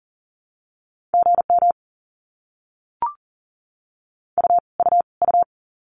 Morse code, heard as a keyed tone, about 40 wpm, 700 Hz. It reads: GM U U U
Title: Abbreviations Repeated 3x - 40wpm